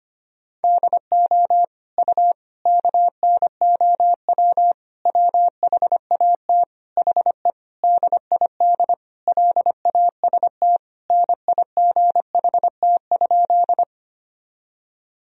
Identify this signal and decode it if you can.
Morse code, heard as a keyed tone, about 25 wpm, 700 Hz. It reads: DO U KNOW WHAT HE DID LAST NIGHT?
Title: Sentences from Top 300 Words and Abbreviations 25wpm